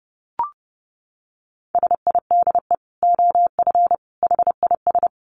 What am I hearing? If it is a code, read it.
Morse code, heard as a keyed tone, about 30 wpm, 700 Hz. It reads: SIDE OF HIS
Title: Dit Challenge 30wpm